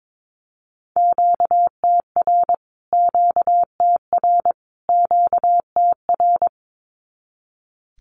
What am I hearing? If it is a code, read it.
Morse code, heard as a keyed tone, about 22 wpm, 700 Hz. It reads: QTR QTR QTR
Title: Q-Codes Repeated 3x 22wpm